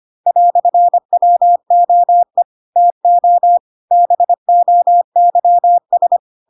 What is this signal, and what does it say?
Morse code, heard as a keyed tone, about 25 wpm, 700 Hz. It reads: "WOE TO BOYS
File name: Pinocchio / Pinocchio ch4 25 25 ik1hge